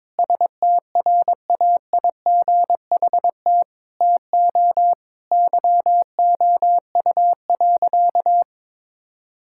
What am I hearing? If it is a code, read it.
Morse code, heard as a keyed tone, about 22 wpm, 700 Hz. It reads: STRAIGHT TO YOU.
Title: Sentences from Top 1000 Words - Review Part 6 - 22wpm